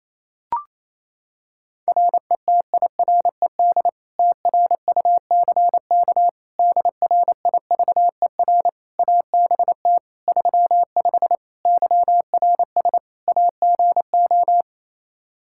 Morse code, heard as a keyed tone, about 28 wpm, 700 Hz. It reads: RETIRED TRUCK DRIVER ABT 35 YRS AGO
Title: Practice QSO - Exchange 3 - More Info - 28wpm